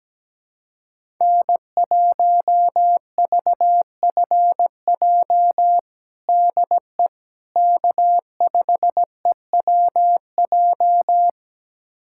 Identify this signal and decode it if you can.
Morse code, heard as a keyed tone, about 17 wpm, 700 Hz. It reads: N1VFJ DE K5EWJ
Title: USA Callsign DE Callsign - 5 Characters - 17wpm